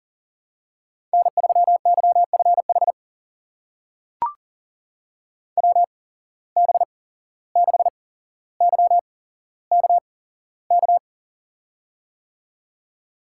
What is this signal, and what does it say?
Morse code, heard as a keyed tone, about 40 wpm, spaced as for 16 wpm, 700 Hz. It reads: N3YFH WB6YKK
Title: USA Call Signs - 8x Character Spacing - 40wpm